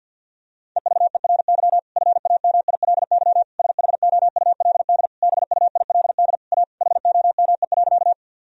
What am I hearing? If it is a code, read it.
Morse code, heard as a keyed tone, about 53 wpm, 700 Hz. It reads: EVERY FAMILY SHOULD BUILD A HOME.
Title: Sentences from Top 1000 Words - Review Part 1 - 50wpm